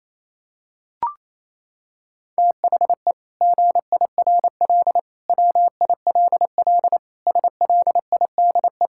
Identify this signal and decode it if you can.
Morse code, heard as a keyed tone, about 28 wpm, 700 Hz. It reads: THE GIRL WILL SLIDE
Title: Sentences from Top 1600 Words 28wpm